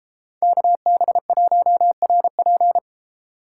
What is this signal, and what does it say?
Morse code, heard as a keyed tone, about 31 wpm, 700 Hz. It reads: KB1RP